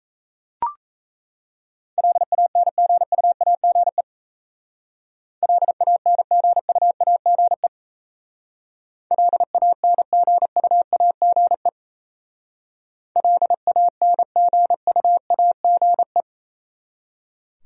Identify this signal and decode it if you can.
Morse code, heard as a keyed tone, about 33 wpm, 700 Hz. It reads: LANGUAGE LANGUAGE LANGUAGE LANGUAGE
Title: Speed-Racing Top 500 Words 28wpm